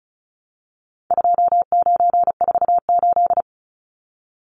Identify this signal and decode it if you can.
Morse code, heard as a keyed tone, about 34 wpm, 700 Hz. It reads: 2948
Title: Numbers - 4-Digit 35wpm